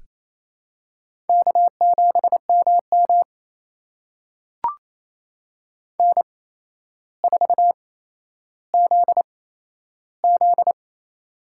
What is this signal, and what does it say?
Morse code, heard as a keyed tone, about 28 wpm, spaced as for 12 wpm, 700 Hz. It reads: K7MM N4ZZ
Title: USA Call Signs - 4 Characters - 8x Character Spacing - 28wpm